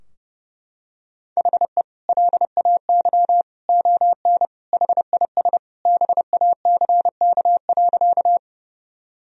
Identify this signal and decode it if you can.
Morse code, heard as a keyed tone, about 30 wpm, 700 Hz. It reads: HE LAY ON HIS BACK.
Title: Sentences from Top 1000 Words - Review Part 2 - 30wpm